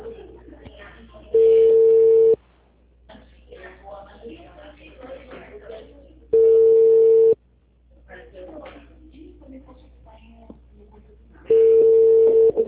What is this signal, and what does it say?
Ringback tone (China, 450 Hz), with speech.
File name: out- -2012- - - .23